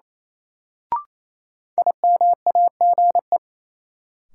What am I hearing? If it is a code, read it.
Morse code, heard as a keyed tone, about 28 wpm, 700 Hz. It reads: IMAGE
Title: Sentences from Top 700 Words - New Word Drill - 28wpm